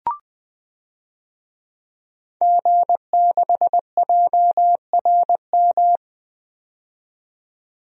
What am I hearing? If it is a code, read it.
Morse code, heard as a keyed tone, about 20 wpm, 700 Hz. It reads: G6JRM
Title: DXCC-100 Call Signs - 5 Characters - 20wpm